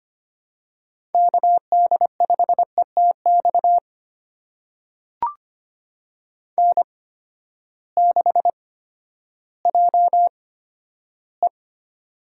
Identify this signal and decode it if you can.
Morse code, heard as a keyed tone, about 25 wpm, spaced as for 12 wpm, 700 Hz. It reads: KD5ETX N6JE
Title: USA Call Signs - 8x Character Spacing - 25wpm